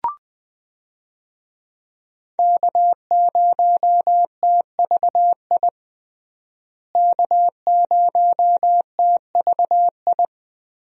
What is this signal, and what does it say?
Morse code, heard as a keyed tone, about 20 wpm, 700 Hz. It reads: K0TVI K0TVI
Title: USA Call Signs - 5 Characters - Repeated 3x - 20wpm